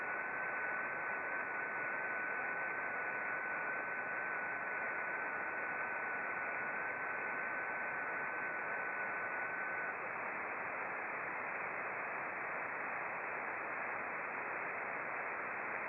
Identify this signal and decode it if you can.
FT8.
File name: contests / sept-vhf-k5tr- -0300